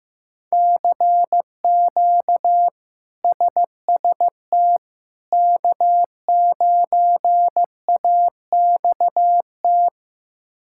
Morse code, heard as a keyed tone, about 15 wpm, 700 Hz. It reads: CQ SST K9AXT
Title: K1USN SST QSO Practice - 15wpm